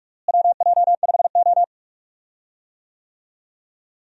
Morse code, heard as a keyed tone, about 46 wpm, 700 Hz. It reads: WJ5O
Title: USA Call Signs - 4 Characters - Repeated 3x - 45wpm